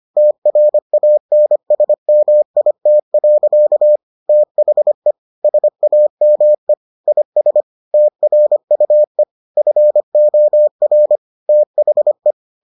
Morse code, heard as a keyed tone, about 25 wpm, 600 Hz. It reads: TRANSMIT. THE SAME IS TRUE FOR THE